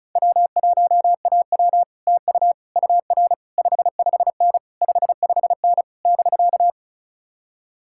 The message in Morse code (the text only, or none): W1AW TU UR 55N 55N <BK>